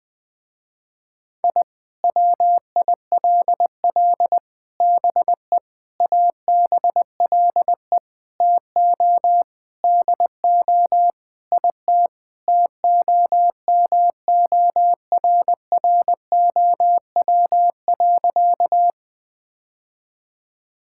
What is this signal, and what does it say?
Morse code, heard as a keyed tone, about 20 wpm, 700 Hz. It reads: I WILL BE ABLE TO DO IT TOMORROW.
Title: Sentences with 4-Word Phrases - Spoken Beginning - 20wpm